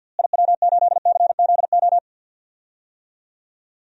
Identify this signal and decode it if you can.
Morse code, heard as a keyed tone, about 53 wpm, 700 Hz. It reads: IW8CZO
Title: DXCC-100 Call Signs - 6 Characters - Repeated 2x - 50wpm